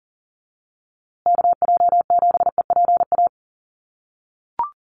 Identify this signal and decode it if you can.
Morse code, heard as a keyed tone, about 40 wpm, 700 Hz. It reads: KJ7EPA
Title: USA Call Signs Repeated 2x 40wpm